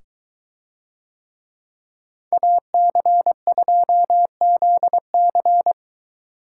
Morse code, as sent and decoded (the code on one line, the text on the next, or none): .- -.-. ..--- --.. -.-.
AC2ZC